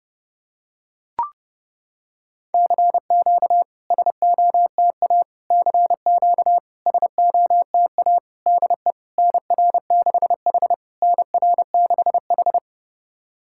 Morse code, heard as a keyed tone, about 30 wpm, 690 Hz. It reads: CQ SOTA CQ SOTA DE NR6H NR6H
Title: SOTA QSO Practice 30wpm